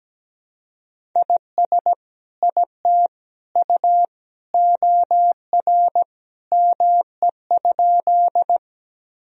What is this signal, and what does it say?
Morse code, heard as a keyed tone, about 17 wpm, 700 Hz. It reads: IS IT U OR ME?
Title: Sentences from Top 200 Words and Abbreviations 17wpm